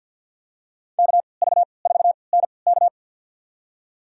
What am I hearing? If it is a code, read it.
Morse code, heard as a keyed tone, about 46 wpm, spaced as for 27 wpm, 690 Hz. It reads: KV4NK
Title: USA Call Signs - 3x Character Spacing - 50wpm